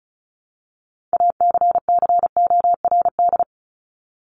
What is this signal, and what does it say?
Morse code, heard as a keyed tone, about 34 wpm, 700 Hz. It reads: ACCORD
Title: ING Suffix - Top 100 Words with Base Words 35wpm